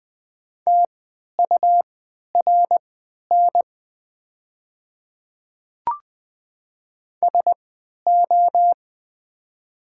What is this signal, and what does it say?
Morse code, heard as a keyed tone, about 20 wpm, spaced as for 11 wpm, 700 Hz. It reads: TURN SO
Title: 3x Character Spacing - Top 200 Words - 20wpm